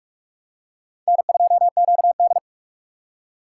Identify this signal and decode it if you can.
Morse code, heard as a keyed tone, about 46 wpm, 690 Hz. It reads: N2QD
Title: USA Call Signs - 3x Character Spacing - 45wpm